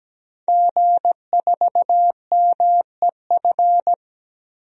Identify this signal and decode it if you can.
Morse code, heard as a keyed tone, about 17 wpm, 700 Hz. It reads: G4MEF
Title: DXCC-100 Call Signs - 5 Characters - Repeated 2x - 17wpm